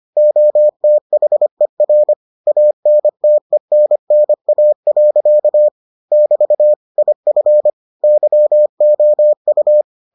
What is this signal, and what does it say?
Morse code, heard as a keyed tone, about 25 wpm, 600 Hz. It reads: OTHER ANTENNA. <BT> IF YOU